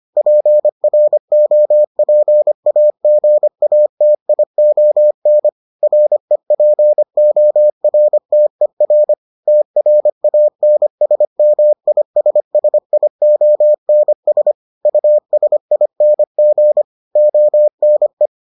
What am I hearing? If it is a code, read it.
Morse code, heard as a keyed tone, about 25 wpm, 600 Hz. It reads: PROPAGATION REPORTER TRANSMISSIONS USING ONE